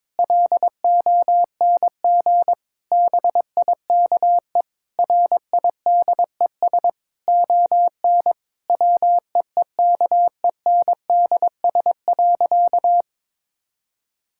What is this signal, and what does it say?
Morse code, heard as a keyed tone, about 22 wpm, 700 Hz. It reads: LONG BIKE RIDES ON WEEKENDS.
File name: Difficult Sentences with Hints - 22wpm